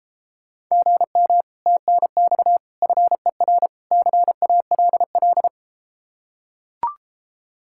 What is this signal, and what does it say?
Morse code, heard as a keyed tone, about 32 wpm, 700 Hz. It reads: GM TNX FER CALL